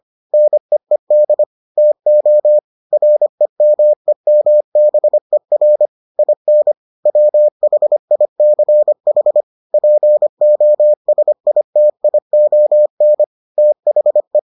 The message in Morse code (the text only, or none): NEED TO REMEMBER IN WHICH POSITION THE